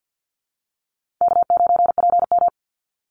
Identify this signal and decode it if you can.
Morse code, heard as a keyed tone, about 50 wpm, 700 Hz. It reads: K9PM
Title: NA SOTA Activator Callsigns 50wpm